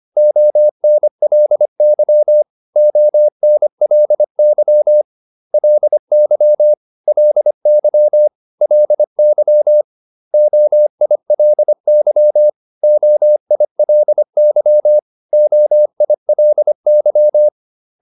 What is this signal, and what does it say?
Morse code, heard as a keyed tone, about 25 wpm, 600 Hz. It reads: ONLY ONLY LY LY LY OILY OILY OILY